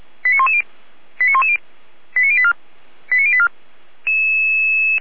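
Five-tone selective call (ZVEI), heard as quick runs of five tones, tones of about 70 ms. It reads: 89110 89110 89084 89084
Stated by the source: Five-tone ZVEI selective call with code 89097, received by SDR.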